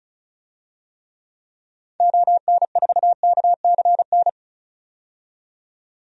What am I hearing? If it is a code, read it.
Morse code, heard as a keyed tone, about 34 wpm, 700 Hz. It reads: ON4KCN
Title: DXCC-100 Call Signs - 6 Characters - 35wpm